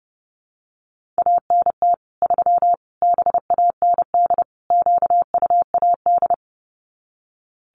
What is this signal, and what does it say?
Morse code, heard as a keyed tone, about 30 wpm, 700 Hz. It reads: ANT 3 BAND QUAD
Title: Common QSO Phrases 30wpm